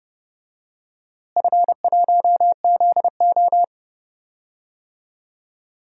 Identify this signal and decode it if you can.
Morse code, heard as a keyed tone, about 31 wpm, 700 Hz. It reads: F1ZO
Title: DXCC-100 Call Signs - 4 Characters - Repeated 2x - 30wpm